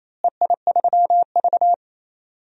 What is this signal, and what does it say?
Morse code, heard as a keyed tone, about 28 wpm, 700 Hz. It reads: EI3V
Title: DXCC-100 Call Signs - 4 Characters - Repeated 2x - 28wpm